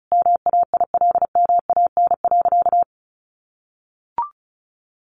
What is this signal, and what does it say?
Morse code, heard as a keyed tone, about 34 wpm, 700 Hz. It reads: MAILMAN.
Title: Easy Phrases 35wpm